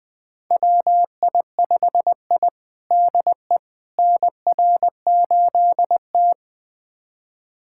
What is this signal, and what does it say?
Morse code, heard as a keyed tone, about 21 wpm, 700 Hz. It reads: WI5I DE NR8T
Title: USA Callsign DE Callsign - 4 Characters - 20wpm